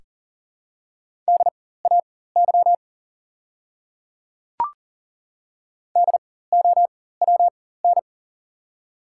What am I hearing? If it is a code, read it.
Morse code, heard as a keyed tone, about 40 wpm, spaced as for 19 wpm, 700 Hz. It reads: DAY DOWN
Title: 4x Character Spacing - Top 100 Words - 40wpm